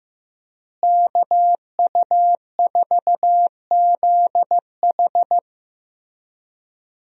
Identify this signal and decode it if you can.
Morse code, heard as a keyed tone, about 15 wpm, 700 Hz. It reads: KU4ZH
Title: USA Call Signs - 5 Characters - Repeated 3x - 15wpm